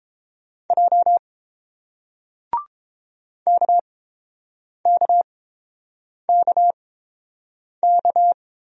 Morse code, heard as a keyed tone, about 30 wpm, 700 Hz. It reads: J K K K K